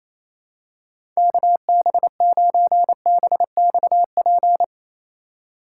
Morse code, heard as a keyed tone, about 27 wpm, 700 Hz. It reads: KB9BXP
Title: USA Call Signs - 3x Character Spacing - 28wpm